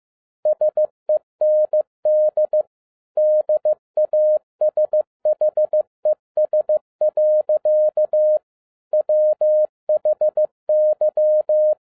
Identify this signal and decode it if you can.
Morse code, heard as a keyed tone, about 15 wpm, 610 Hz. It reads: SEND DASHES. WHY